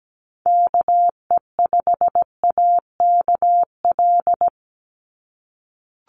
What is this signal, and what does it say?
Morse code, heard as a keyed tone, about 17 wpm, 700 Hz. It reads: KE5AKL
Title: NA SOTA Activator Callsigns 17wpm